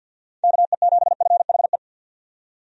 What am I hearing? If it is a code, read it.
Morse code, heard as a keyed tone, about 49 wpm, 700 Hz. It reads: KE7FHE
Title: USA Call Signs - 6 Characters - 50wpm